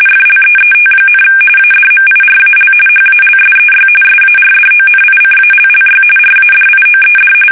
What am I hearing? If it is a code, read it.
RTTY, about 45.45 baud. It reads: FLEET, ADMIRAL HALSEY SARDONICALLY RADIOED
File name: fsk850 / battleshipnj